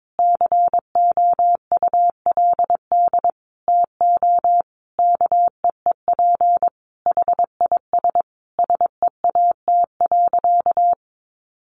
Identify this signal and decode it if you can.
Morse code, heard as a keyed tone, about 22 wpm, 700 Hz. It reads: COULD TO KEEP HIS SEAT.
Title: Sentences from Top 1500 Words 22wpm